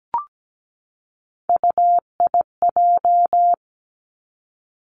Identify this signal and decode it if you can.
Morse code, heard as a keyed tone, about 17 wpm, 700 Hz. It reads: UIJ
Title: Three Challenging Letter-Number Combinations 17wpm